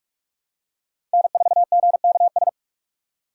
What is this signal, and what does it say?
Morse code, heard as a keyed tone, about 46 wpm, 700 Hz. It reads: N4GKS
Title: USA Call Signs - 5 Characters - 3x Character Spacing - 45wpm